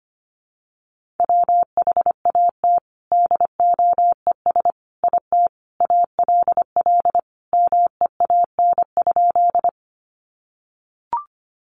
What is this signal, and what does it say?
Morse code, heard as a keyed tone, about 25 wpm, 700 Hz. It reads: WHAT DOES IT ALL MEAN?
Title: Sentences from Top 200 Words 25wpm